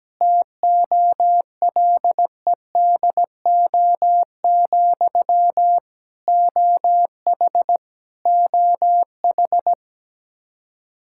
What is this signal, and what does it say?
Morse code, heard as a keyed tone, about 17 wpm, 700 Hz. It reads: TOLEDO, OH OH
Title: Practice QSO - Exchange 1 - Beginning - 17wpm